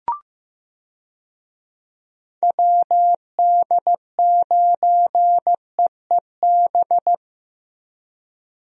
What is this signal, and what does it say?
Morse code, heard as a keyed tone, about 15 wpm, 700 Hz. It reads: WD9EEB
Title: USA Call Signs - 6 Characters - 15wpm